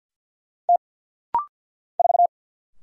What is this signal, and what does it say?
Morse code, heard as a keyed tone, about 50 wpm, 700 Hz. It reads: T 4
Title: Single Letter-Number - Rapid-Fire 50wpm